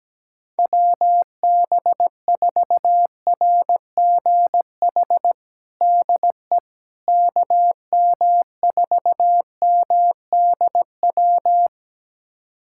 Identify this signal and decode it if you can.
Morse code, heard as a keyed tone, about 17 wpm, 700 Hz. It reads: WB4RGH DE KM4MDW
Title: USA Callsign DE Callsign 17wpm